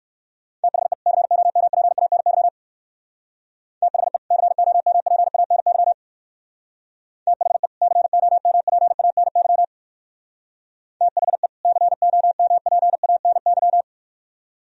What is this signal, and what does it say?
Morse code, heard as a keyed tone, about 58 wpm, 700 Hz. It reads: THE COMPANY THE COMPANY THE COMPANY THE COMPANY